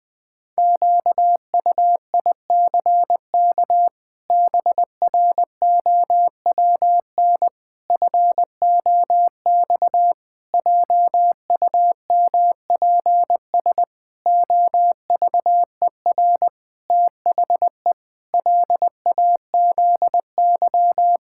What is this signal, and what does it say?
Morse code, heard as a keyed tone, about 20 wpm, 700 Hz. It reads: QUICK BROWN FOX JUMPS OVER THE LAZY
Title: Sets of 8 Words - 20wpm